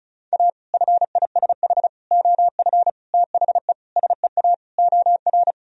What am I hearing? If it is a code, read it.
Morse code, heard as a keyed tone, about 36 wpm, 700 Hz. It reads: A FISH OF THE SEA OR
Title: Sentences from Top 1000 Words - Review Part 1 - 35wpm